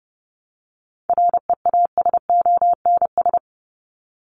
Morse code, heard as a keyed tone, about 30 wpm, 700 Hz. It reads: REASONS
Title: ES:S Suffix - Top 100 Words with Base Words 30wpm